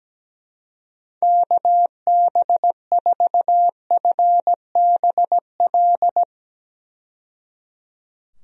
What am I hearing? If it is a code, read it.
Morse code, heard as a keyed tone, about 17 wpm, 700 Hz. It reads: KB4FBL
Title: USA Call Signs - 6 Characters - 17wpm